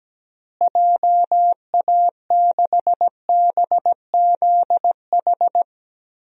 Morse code, heard as a keyed tone, about 17 wpm, 700 Hz. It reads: JA6BZH
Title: DXCC-100 Call Signs - 6 Characters - Repeated 2x - 17wpm